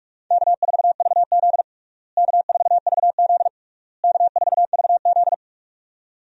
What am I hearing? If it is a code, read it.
Morse code, heard as a keyed tone, about 41 wpm, 700 Hz. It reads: K4VZ K4VZ K4VZ
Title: USA Call Signs - 4 Characters - Repeated 3x - 45wpm